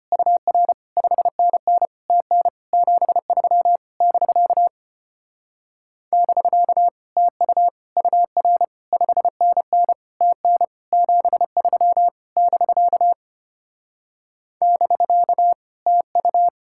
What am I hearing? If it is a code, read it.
Morse code, heard as a keyed tone, about 33 wpm, 700 Hz. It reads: UR 5NN TN 73 <BK> <BK> TU UR 5NN TN 73 <BK> <BK> TU